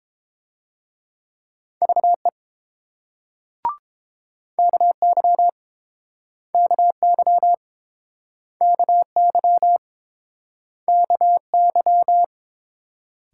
Morse code, heard as a keyed tone, about 30 wpm, 700 Hz. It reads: VE KY KY KY KY